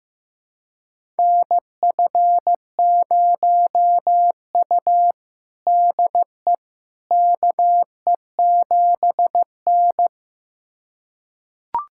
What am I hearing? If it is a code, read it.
Morse code, heard as a keyed tone, about 15 wpm, 700 Hz. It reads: NF0U DE KE7N
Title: USA Callsign DE Callsign - 4 Characters - 15wpm